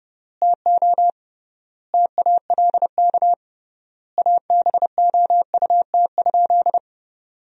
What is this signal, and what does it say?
Morse code, heard as a keyed tone, about 30 wpm, 700 Hz. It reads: TO TALK ABOUT?
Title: 3x Word-Spacing - Sentences from Top 400 Words - 30wpm